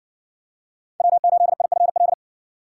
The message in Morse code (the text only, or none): W8IFL